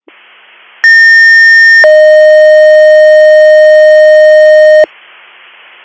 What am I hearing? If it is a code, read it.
Two-tone page, heard as a one-second tone, then a three-second tone, about 1800 then 620 Hz. It is heard from 0.8 to 4.8 s.